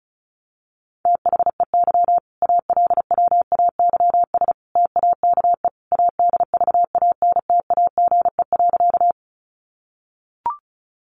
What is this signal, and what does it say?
Morse code, heard as a keyed tone, about 34 wpm, 700 Hz. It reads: THEY ALWAYS TAKE ADVANTAGE.
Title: Sentences from Top 900 Words 35wpm